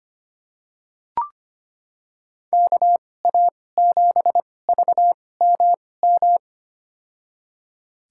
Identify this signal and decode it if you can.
Morse code, heard as a keyed tone, about 25 wpm, spaced as for 18 wpm, 700 Hz. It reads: KA7VMM
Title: USA Call Signs - 6 Characters - 2x Character Spacing - 25wpm